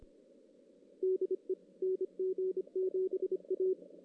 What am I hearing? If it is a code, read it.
Morse code, heard as a keyed tone, about 26 wpm, 410 Hz. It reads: DE NG7A
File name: 2016 JAN CW NAQP / 2016 Jan naqp k5ot - 0-5